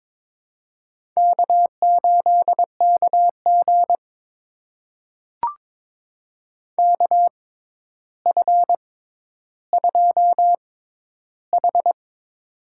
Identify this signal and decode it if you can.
Morse code, heard as a keyed tone, about 22 wpm, spaced as for 11 wpm, 700 Hz. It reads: K8KG KF2H
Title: USA Call Signs - 4 Characters - 6x Character Spacing - 22wpm